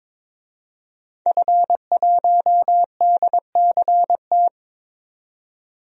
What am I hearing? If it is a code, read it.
Morse code, heard as a keyed tone, about 22 wpm, 700 Hz. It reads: F1DCT
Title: DXCC-100 Call Signs - 5 Characters - 22wpm